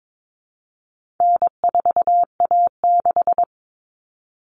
Morse code, heard as a keyed tone, about 22 wpm, 700 Hz. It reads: N4A6
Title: Common Cut Numbers - 4-Digit 22wpm